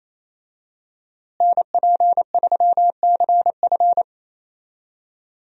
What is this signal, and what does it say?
Morse code, heard as a keyed tone, about 28 wpm, 700 Hz. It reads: NP3CF
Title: USA Call Signs - 5 Characters - Repeated 2x - 28wpm